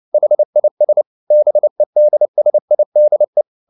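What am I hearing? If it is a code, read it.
Morse code, heard as a keyed tone, about 29 wpm, 600 Hz. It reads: HIS BEDSIDE
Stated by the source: Morse code at 29 wpm, sending Ken Follett's novel Never in English, from CW med Gnister.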